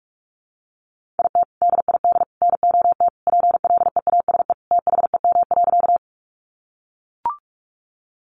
Morse code, heard as a keyed tone, about 44 wpm, 700 Hz. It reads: IT DID NOT PLEASE THEM.
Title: Sentences from Top 1500 Words 45wpm